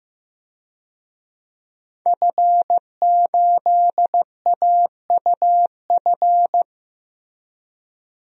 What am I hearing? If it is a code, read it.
Morse code, heard as a keyed tone, about 16 wpm, 700 Hz. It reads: F8AUF
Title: DXCC-100 Call Signs - 5 Characters - Repeated 2x - 15wpm